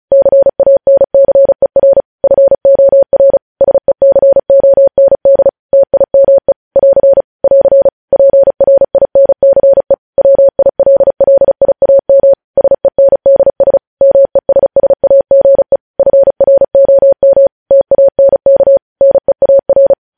Morse code, heard as a keyed tone, about 36 wpm, 550 Hz. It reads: CANCER FOR SECOND TIME <AR> <AR> PRINCE WILLIAM SENDS MESSAGE FROM TANK NEAR
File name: podcast / News headlines-1x-35x35